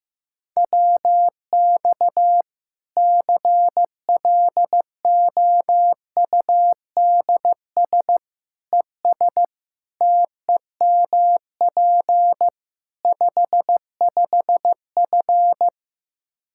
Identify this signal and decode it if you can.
Morse code, heard as a keyed tone, about 15 wpm, 700 Hz. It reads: WX CLOUDS ES TEMP 55F
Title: K1USN SST QSO Practice 15wpm